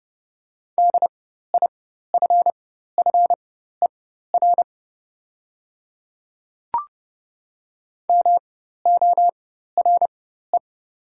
Morse code, heard as a keyed tone, about 30 wpm, spaced as for 14 wpm, 700 Hz. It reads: DIFFER MORE
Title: 4x Character Spacing - Top 200 Words - 30wpm